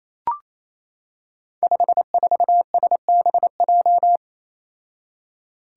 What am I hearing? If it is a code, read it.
Morse code, heard as a keyed tone, about 28 wpm, 700 Hz. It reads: 54SBJ
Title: Five Challenging Letter-Number Combinations 28wpm